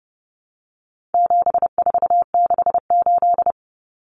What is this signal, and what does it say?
Morse code, heard as a keyed tone, about 30 wpm, 700 Hz. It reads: 7468
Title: Numbers - 4-Digit 30wpm